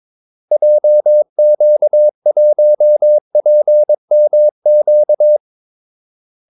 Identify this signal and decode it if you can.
Morse code, heard as a keyed tone, about 22 wpm, 600 Hz. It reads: JQ1PMQ
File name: Callsigns-22wpm